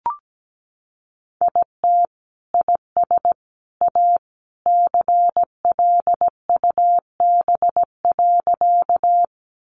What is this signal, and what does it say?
Morse code, heard as a keyed tone, about 17 wpm, 700 Hz. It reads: IT IS A CLUB.
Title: Sentences from Top 1200 Words 17wpm